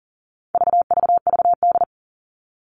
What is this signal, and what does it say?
Morse code, heard as a keyed tone, about 40 wpm, 700 Hz. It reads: VVVD
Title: All Cut Numbers - 4-Digit 40wpm